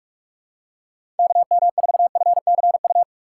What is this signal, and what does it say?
Morse code, heard as a keyed tone, about 46 wpm, 700 Hz. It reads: KM4FCU
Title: USA Call Signs - 4x Character Spacing - 45wpm